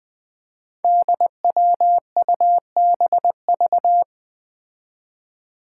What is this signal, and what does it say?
Morse code, heard as a keyed tone, about 20 wpm, 700 Hz. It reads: DWUBV
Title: Five Challenging Letter-Number Combinations 20wpm